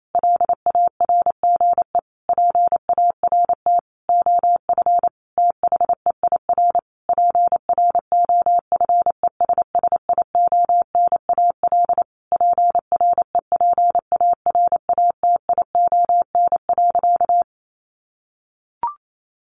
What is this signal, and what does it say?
Morse code, heard as a keyed tone, about 28 wpm, 700 Hz. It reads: LARGE PART OF THEIR PROFESSIONAL PREPARATION.
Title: Sentences from Top 1400 Words 28wpm